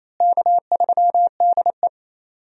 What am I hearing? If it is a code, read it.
Morse code, heard as a keyed tone, about 28 wpm, 700 Hz. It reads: K3DE
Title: USA Call Signs - 4 Characters - Repeated 2x - 28wpm